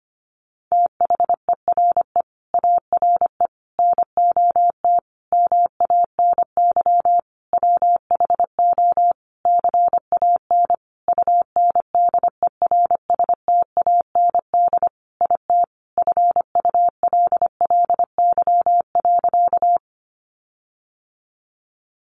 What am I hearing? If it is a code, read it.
Morse code, heard as a keyed tone, about 25 wpm, 700 Hz. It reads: THERE ARE NOT MANY WHO CAN UNDERSTAND IT FULLY.
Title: Sentences from Top 1300 Words 25wpm